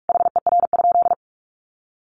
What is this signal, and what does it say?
Morse code, heard as a keyed tone, about 44 wpm, 700 Hz. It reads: HER?
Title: Sentences from Top 1300 Words 45wpm